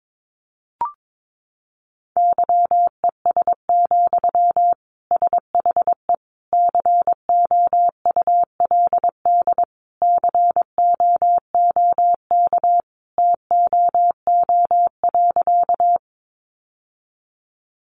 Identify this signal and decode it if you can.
Morse code, heard as a keyed tone, about 22 wpm, 700 Hz. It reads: YES, SHE COULD COOK TOO.
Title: Sentences from Top 700 Words 22wpm